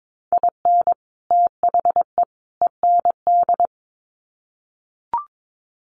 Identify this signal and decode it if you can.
Morse code, heard as a keyed tone, about 22 wpm, 700 Hz. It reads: IN THE END
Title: Sets of 3 Words 22wpm